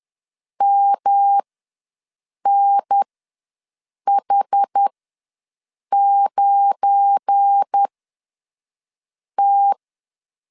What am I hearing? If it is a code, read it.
Morse code, heard as a keyed tone, about 10 wpm, spaced as for 5 wpm, 800 Hz. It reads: MNH9T
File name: LEZIONI / 7 GF9Z / 7 GF9Z ws 10wpm 3919